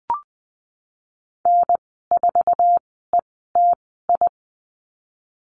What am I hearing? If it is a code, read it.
Morse code, heard as a keyed tone, about 20 wpm, spaced as for 15 wpm, 700 Hz. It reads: N4ETI
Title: 048 Lesson 9g - Number 4 - Call Signs 20wpm